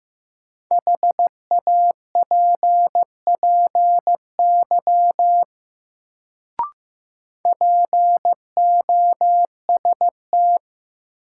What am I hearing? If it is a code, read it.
Morse code, heard as a keyed tone, about 15 wpm, 700 Hz. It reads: HAPPY POST
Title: Sentences from Top 1000 Words - New Word Drill - 15wpm